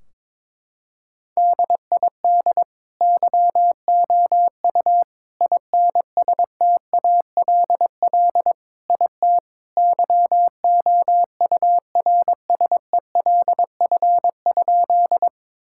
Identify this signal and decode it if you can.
Morse code, heard as a keyed tone, about 22 wpm, 700 Hz. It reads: DID YOU INSTALL IT YOURSELF?
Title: Sentences from Top 1600 Words 22wpm